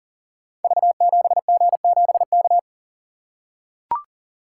Morse code, heard as a keyed tone, about 40 wpm, 700 Hz. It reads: V7G7K
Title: Code Groups - 5 Letter:Numbers - 40wpm